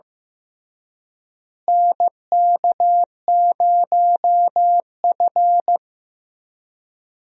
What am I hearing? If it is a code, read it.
Morse code, heard as a keyed tone, about 16 wpm, 700 Hz. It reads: NK0F
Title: USA Call Signs - 4 Characters - Repeated 3x - 15wpm